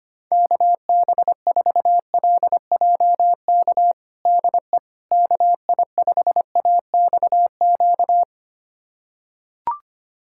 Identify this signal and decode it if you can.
Morse code, heard as a keyed tone, about 25 wpm, 700 Hz. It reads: KB4LJK DE KI5AXQ
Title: USA Callsign DE Callsign - 6 Characters - 25wpm